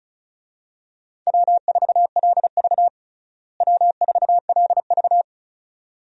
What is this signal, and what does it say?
Morse code, heard as a keyed tone, about 34 wpm, 700 Hz. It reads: W4LV W4LV
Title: USA Call Signs - 4 Characters - Repeated 2x - 35wpm